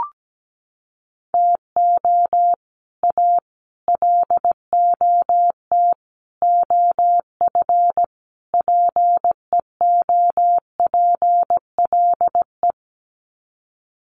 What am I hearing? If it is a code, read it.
Morse code, heard as a keyed tone, about 17 wpm, 700 Hz. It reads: TO A LOT OF PEOPLE
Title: Sets of 5 Words Encore 17wpm